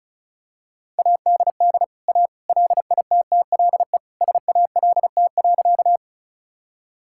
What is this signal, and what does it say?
Morse code, heard as a keyed tone, about 34 wpm, 700 Hz. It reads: ADD A LITTLE SALT.
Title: Sentences from Top 1000 Words - Review Part 5 - 35wpm